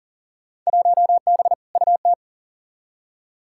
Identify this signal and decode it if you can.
Morse code, heard as a keyed tone, about 40 wpm, 700 Hz. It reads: 1B UT
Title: Field Day Exchanges - 40wpm